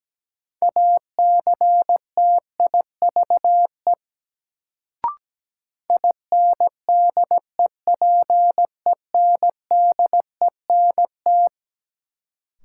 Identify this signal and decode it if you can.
Morse code, heard as a keyed tone, about 17 wpm, 700 Hz. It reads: ACTIVE INDEPENDENT
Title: Sentences from Top 900 Words - New Word Drill - 17wpm